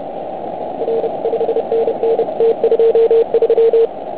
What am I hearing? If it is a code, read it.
Morse code, heard as a keyed tone, about 29 wpm, 470 Hz. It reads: R 5NN T23
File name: Archiv / CUC149 / OK1IF CUC149